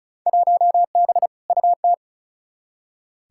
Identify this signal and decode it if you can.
Morse code, heard as a keyed tone, about 36 wpm, 700 Hz. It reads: 1B UT
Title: Field Day Exchanges - 35wpm